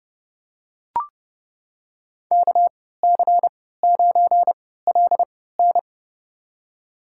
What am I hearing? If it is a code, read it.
Morse code, heard as a keyed tone, about 30 wpm, spaced as for 17 wpm, 700 Hz. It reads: KC9LN
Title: USA Call Signs - 3x Character Spacing - 30wpm